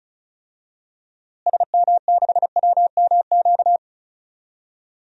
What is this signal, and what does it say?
Morse code, heard as a keyed tone, about 34 wpm, 700 Hz. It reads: SM6WMQ
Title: DXCC-100 Call Signs - 6 Characters - Repeated 3x - 35wpm